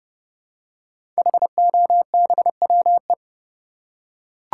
Morse code, heard as a keyed tone, about 31 wpm, 700 Hz. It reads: HOBWE
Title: Code Groups - 5 Letters - 30wpm